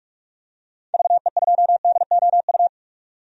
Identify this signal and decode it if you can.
Morse code, heard as a keyed tone, about 48 wpm, 700 Hz. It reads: VE2DOU
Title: DXCC-100 Call Signs - 6 Characters - 45wpm